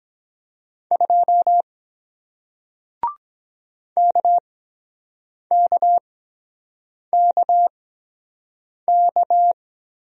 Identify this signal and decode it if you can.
Morse code, heard as a keyed tone, about 22 wpm, spaced as for 7 wpm, 700 Hz. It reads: 2 KKKK